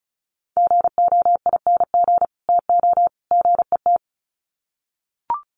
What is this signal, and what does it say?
Morse code, heard as a keyed tone, about 34 wpm, 700 Hz. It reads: GOING TO GET
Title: Sets of 3 Words 35wpm